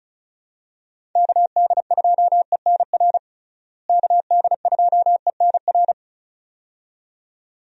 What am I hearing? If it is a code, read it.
Morse code, heard as a keyed tone, about 34 wpm, 700 Hz. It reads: KD2ENR KD2ENR
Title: USA Call Signs - 6 Characters - Repeated 2x - 35wpm